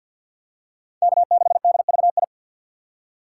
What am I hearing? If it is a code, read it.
Morse code, heard as a keyed tone, about 50 wpm, 700 Hz. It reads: K6DVI
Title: USA Call Signs - 5 Characters - 4x Character Spacing - 50wpm